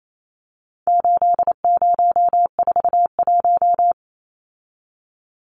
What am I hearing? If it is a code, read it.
Morse code, heard as a keyed tone, about 28 wpm, 700 Hz. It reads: 8041
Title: Numbers - 4-Digit 28wpm